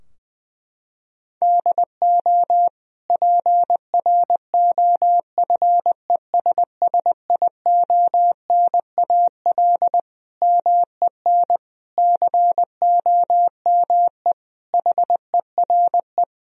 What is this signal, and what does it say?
Morse code, heard as a keyed tone, about 20 wpm, 700 Hz. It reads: DO PROFESSIONAL MEN COME HERE
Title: Sentences from Top 1000 Words - Review Part 4 - 20wpm